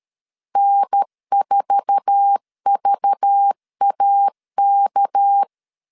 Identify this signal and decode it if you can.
Morse code, heard as a keyed tone, about 13 wpm, 800 Hz. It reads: N4VAK
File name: LEZIONI / 8 YJ45 / 8 YJ4 12wpm 2151